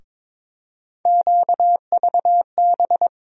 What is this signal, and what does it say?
Morse code, heard as a keyed tone, about 22 wpm, 700 Hz. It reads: QVB
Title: USA Call Sign Postfixes - 3 Letters 22wpm